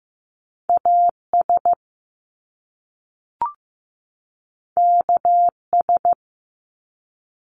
Morse code, heard as a keyed tone, about 15 wpm, 700 Hz. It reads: AS KS
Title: Top Two Letter Combinations in Words 15wpm